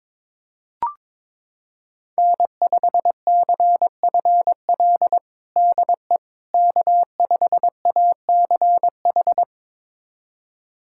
Morse code, heard as a keyed tone, about 22 wpm, 700 Hz. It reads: N5CFL DE K5ACH
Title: USA Callsign DE Callsign - 5 Characters - 22wpm